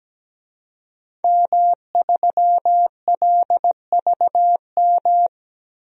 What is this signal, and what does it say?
Morse code, heard as a keyed tone, about 17 wpm, 700 Hz. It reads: M3LVM
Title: DXCC-100 Call Signs - 5 Characters - Repeated 2x - 17wpm